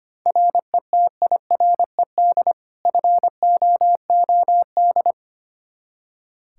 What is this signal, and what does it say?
Morse code, heard as a keyed tone, about 25 wpm, 700 Hz. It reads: RETIRED FOOD
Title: Practice QSO - Exchange 3 - More Info - 25wpm